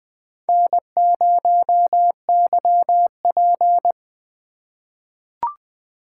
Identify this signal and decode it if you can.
Morse code, heard as a keyed tone, about 20 wpm, 700 Hz. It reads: N0YP
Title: USA Call Signs - 4 Characters - 2x Character Spacing - 20wpm